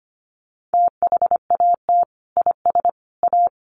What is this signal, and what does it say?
Morse code, heard as a keyed tone, about 25 wpm, 700 Hz. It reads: THAT IS A
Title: Sentences from Top 1500 Words 25wpm